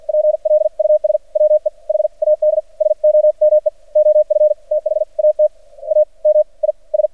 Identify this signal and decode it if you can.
Morse code, heard as a keyed tone, about 53 wpm, spaced as for 40 wpm, 610 Hz. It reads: PLUS WE HAD SOME OF THAT AMIS
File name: archived / CRSnet